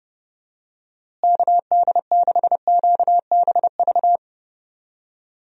Morse code, heard as a keyed tone, about 31 wpm, 700 Hz. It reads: KD6QBV
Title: USA Call Signs - 6 Characters - Repeated 2x - 30wpm